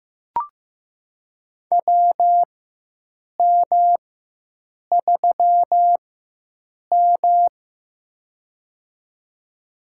Morse code, heard as a keyed tone, about 15 wpm, spaced as for 7 wpm, 700 Hz. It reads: WM3M
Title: USA Call Signs - 4 Characters - 4x Character Spacing - 15wpm